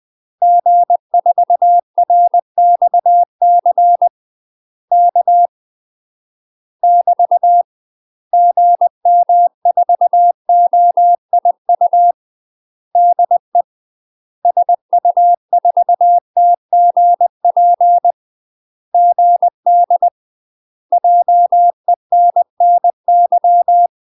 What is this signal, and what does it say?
Morse code, heard as a keyed tone, about 20 wpm, 700 Hz. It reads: G4RXC K <BT> GM4OIU DE SU4TGP GD JENNY
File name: UK MASTER QSOs TRAINING / 20WPM / UK MASTER QSOs 0071